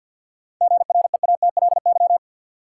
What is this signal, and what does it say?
Morse code, heard as a keyed tone, about 53 wpm, 700 Hz. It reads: GREATLY
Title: Sentences from Top 1400 Words - New Word Drill - 50wpm